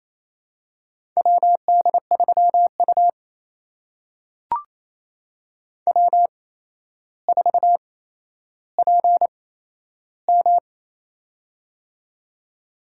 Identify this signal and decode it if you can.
Morse code, heard as a keyed tone, about 28 wpm, spaced as for 12 wpm, 700 Hz. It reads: WD3U W4PM
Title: USA Call Signs - 4 Characters - 8x Character Spacing - 28wpm